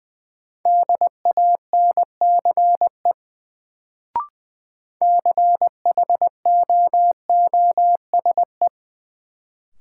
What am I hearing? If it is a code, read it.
Morse code, heard as a keyed tone, about 21 wpm, 700 Hz. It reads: DANCE CHOOSE
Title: Sentences from Top 700 Words - New Word Drill - 20wpm